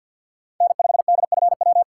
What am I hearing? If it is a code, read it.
Morse code, heard as a keyed tone, about 53 wpm, 700 Hz. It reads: N5DFW
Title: USA Call Signs - 5 Characters - 3x Character Spacing - 50wpm